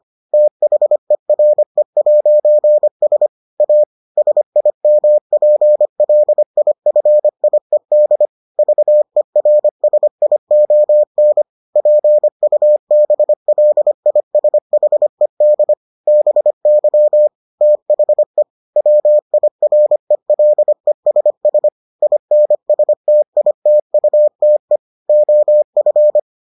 Morse code, heard as a keyed tone, about 25 wpm, 600 Hz. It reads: THERE'S A SIMPLIFIED VERSION PUBLISHED BY THE WIRELESS INSTITUTE OF